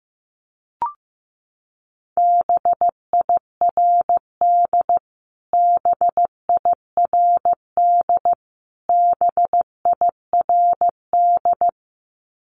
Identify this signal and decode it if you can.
Morse code, heard as a keyed tone, about 15 wpm, 700 Hz. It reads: BIRD BIRD BIRD
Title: Top 500 Words Repeated 15wpm